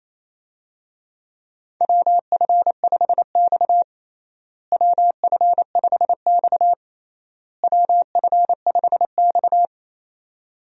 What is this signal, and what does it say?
Morse code, heard as a keyed tone, about 28 wpm, 700 Hz. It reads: WF5X WF5X WF5X
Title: USA Call Signs - 4 Characters - Repeated 3x - 28wpm